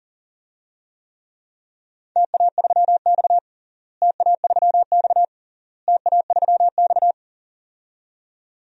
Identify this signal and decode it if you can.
Morse code, heard as a keyed tone, about 40 wpm, 700 Hz. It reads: TA3X TA3X TA3X
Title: DXCC-100 Call Signs - 4 Characters - Repeated 3x - 40wpm